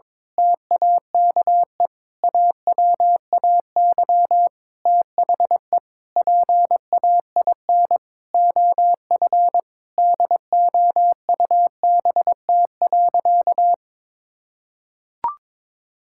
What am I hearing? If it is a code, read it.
Morse code, heard as a keyed tone, about 22 wpm, 700 Hz. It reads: TAKE AWAY THE PAIN OF DOUBT.
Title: Sentences from Top 1600 Words 22wpm